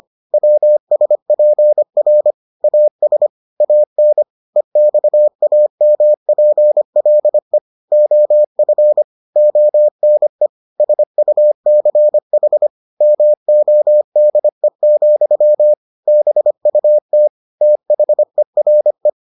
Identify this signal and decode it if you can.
Morse code, heard as a keyed tone, about 25 wpm, 600 Hz. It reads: WSPR AS AN EXAMPLE OF ONE SUCH MODE, BUT THERE